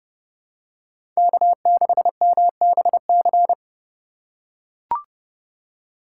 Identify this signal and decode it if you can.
Morse code, heard as a keyed tone, about 30 wpm, 700 Hz. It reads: K6MBC